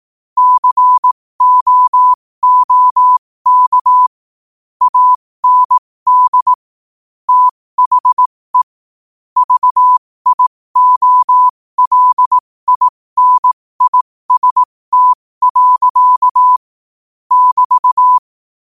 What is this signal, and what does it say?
Morse code, heard as a keyed tone, about 18 wpm, spaced as for 15 wpm, 1000 Hz. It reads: COOK AND THE VIOLINIST. <BT>
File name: qotdfiles / 15wpm